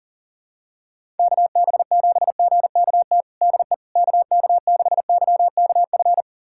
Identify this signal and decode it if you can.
Morse code, heard as a keyed tone, about 37 wpm, 700 Hz. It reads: KB7GKT DE KK6YKF
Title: USA Callsign DE Callsign 40wpm